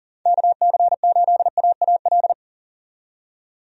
Morse code, heard as a keyed tone, about 40 wpm, 700 Hz. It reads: KC8AAL
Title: USA Call Signs - 6 Characters - 40wpm